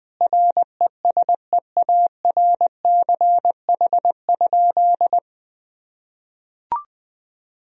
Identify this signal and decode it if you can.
Morse code, heard as a keyed tone, about 20 wpm, 700 Hz. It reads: RESEARCH?
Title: Sentences from Top 1000 Words - Review Part 4 - 20wpm